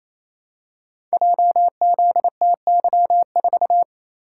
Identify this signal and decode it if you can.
Morse code, heard as a keyed tone, about 28 wpm, 700 Hz. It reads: JZTY4
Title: Code Groups - 5 Letter:Numbers - 28wpm